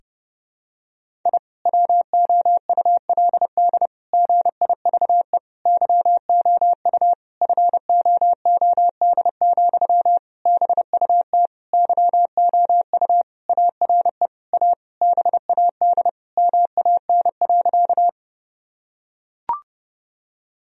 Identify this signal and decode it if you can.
Morse code, heard as a keyed tone, about 30 wpm, 700 Hz. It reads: I WOULD GIVE YOU FOOD, BUT YOU ARE A BAD MAN.
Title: Sentences from Top 1000 Words - Review Part 1 - 30wpm